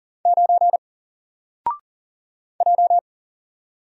Morse code, heard as a keyed tone, about 40 wpm, 700 Hz. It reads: 9 J
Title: Single Character (Letter:Number:Punctuation:Prosign) Alt 40wpm